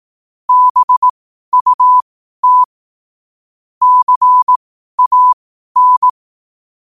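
Morse code, heard as a keyed tone, about 18 wpm, spaced as for 12 wpm, 1000 Hz. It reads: BUT CAN
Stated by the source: 12 wpm Morse.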